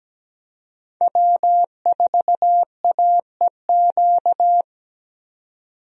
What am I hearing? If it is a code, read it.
Morse code, heard as a keyed tone, about 17 wpm, 700 Hz. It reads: W4AEQ
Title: USA Call Signs - 5 Characters - 17wpm